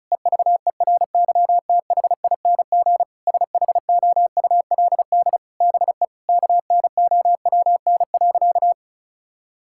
Morse code, heard as a keyed tone, about 36 wpm, 700 Hz. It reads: EVERYTHING SHOULD BE KNOWN.
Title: Sentences from Top 1000 Words - Review Part 6 - 35wpm